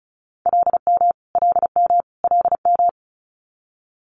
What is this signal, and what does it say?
Morse code, heard as a keyed tone, about 36 wpm, 700 Hz. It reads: LM LM LM
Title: Multi-letter Phonemes Repeated 3x 35wpm